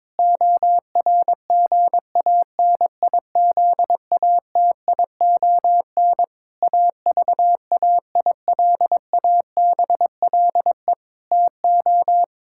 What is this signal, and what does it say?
Morse code, heard as a keyed tone, about 22 wpm, 700 Hz. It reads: ORGANIZATION AVAILABLE TO
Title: Sentences from Top 1000 Words - Review Part 5 - 22wpm